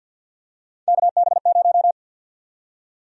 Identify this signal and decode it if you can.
Morse code, heard as a keyed tone, about 52 wpm, 700 Hz. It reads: KB0
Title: USA Call Sign Prefixes - 50wpm